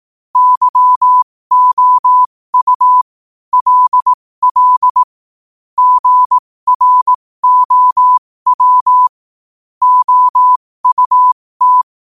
Morse code, heard as a keyed tone, about 18 wpm, spaced as for 15 wpm, 1000 Hz. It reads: YOU LL GROW OUT
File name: qotdfiles / 15wpm